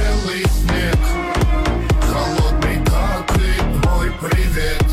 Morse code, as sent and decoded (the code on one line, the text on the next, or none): none